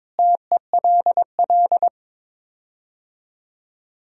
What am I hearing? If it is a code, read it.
Morse code, heard as a keyed tone, about 22 wpm, 700 Hz. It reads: TELL
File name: practice / 22wpm / Sets of 6 Words - 22wpm